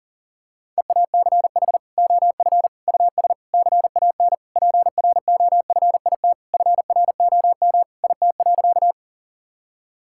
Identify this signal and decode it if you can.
Morse code, heard as a keyed tone, about 40 wpm, 700 Hz. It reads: EACH OF US CAN PROFIT FROM IT.
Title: Sentences from Top 1000 Words - Review Part 5 - 40wpm